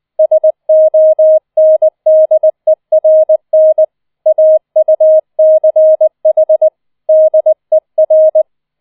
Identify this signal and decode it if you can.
Morse code, heard as a keyed tone, about 20 wpm, 630 Hz. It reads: SONDERN AUCH DER
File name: 004 Vierter Streich 20wpm